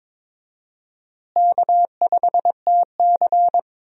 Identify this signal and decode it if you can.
Morse code, heard as a keyed tone, about 22 wpm, 700 Hz. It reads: K5TC
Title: USA Call Signs - 4 Characters - Repeated 2x - 22wpm